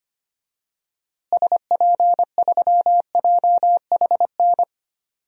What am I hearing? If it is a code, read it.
Morse code, heard as a keyed tone, about 25 wpm, 700 Hz. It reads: SP3JHN
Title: DXCC-100 Call Signs - 6 Characters - Repeated 2x - 25wpm